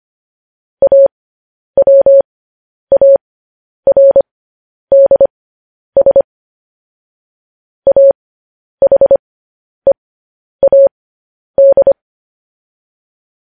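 Morse code, heard as a keyed tone, about 25 wpm, spaced as for 10 wpm, 550 Hz. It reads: AWARDS AHEAD
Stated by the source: Morse code news headlines transmission.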